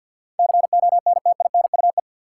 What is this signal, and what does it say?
Morse code, heard as a keyed tone, about 48 wpm, 700 Hz. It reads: CONTINUE
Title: Sentences from Top 1100 Words - New Word Drill Alt - 50wpm